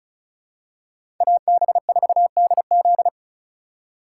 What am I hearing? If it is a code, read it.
Morse code, heard as a keyed tone, about 36 wpm, 700 Hz. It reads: AB4DZ
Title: USA Call Signs - 5 Characters - 6x Character Spacing - 35wpm